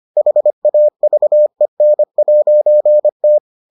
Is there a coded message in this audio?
Yes: Morse code from 0.2 to 3.4 s, heard as a keyed tone, about 25 wpm, 600 Hz. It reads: HAVEN'T